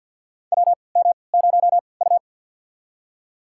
Morse code, heard as a keyed tone, about 51 wpm, spaced as for 28 wpm, 700 Hz. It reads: WM0U
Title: USA Call Signs - 4 Characters - 3x Character Spacing - 50wpm